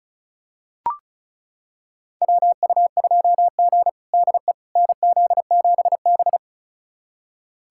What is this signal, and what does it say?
Morse code, heard as a keyed tone, about 36 wpm, 700 Hz. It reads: WU2G DE NZ7B
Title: USA Callsign DE Callsign - 4 Characters - 35wpm